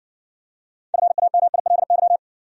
Morse code, heard as a keyed tone, about 55 wpm, 700 Hz. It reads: FAMILY